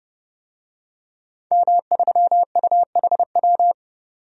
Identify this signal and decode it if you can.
Morse code, heard as a keyed tone, about 30 wpm, 700 Hz. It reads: M3UHW
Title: DXCC-100 Call Signs - 5 Characters - 30wpm